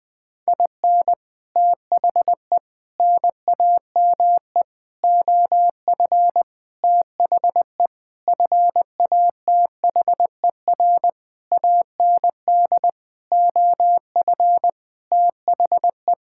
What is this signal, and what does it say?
Morse code, heard as a keyed tone, about 20 wpm, 700 Hz. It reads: IN THE NAME OF THE FATHER AND OF THE
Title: Sentences with 9-Word Phrases - Spoken Beginning - 20wpm